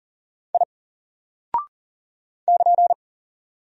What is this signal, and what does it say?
Morse code, heard as a keyed tone, about 40 wpm, 700 Hz. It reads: I <KN>
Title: Single Character (Letter:Number:Punctuation:Prosign) Alt 40wpm